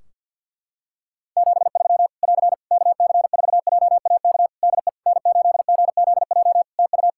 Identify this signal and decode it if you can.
Morse code, heard as a keyed tone, about 52 wpm, 700 Hz. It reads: 73 <AR> KC4JAK DE N8GZJ TU
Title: Practice QSO - Exchange 4 - Ending - 50wpm